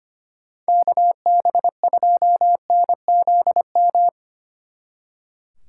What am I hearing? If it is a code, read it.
Morse code, heard as a keyed tone, about 30 wpm, 700 Hz. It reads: KB2NZM
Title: Speed Racing - USA 2x3 Call Signs 25wpm